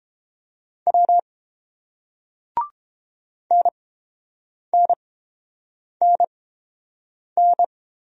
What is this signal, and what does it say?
Morse code, heard as a keyed tone, about 30 wpm, spaced as for 7 wpm, 690 Hz. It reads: W NNNN